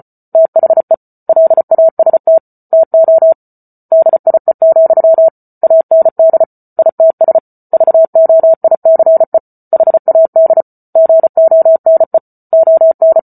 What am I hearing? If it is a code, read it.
Morse code, heard as a keyed tone, about 34 wpm, 650 Hz. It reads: THE LAST TO DIE, AND ITS VOICE HAD GONE ON